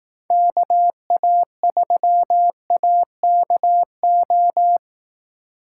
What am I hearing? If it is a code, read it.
Morse code, heard as a keyed tone, about 18 wpm, 700 Hz. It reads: KA3AKO
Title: Speed Racing - USA 2x3 Call Signs 15wpm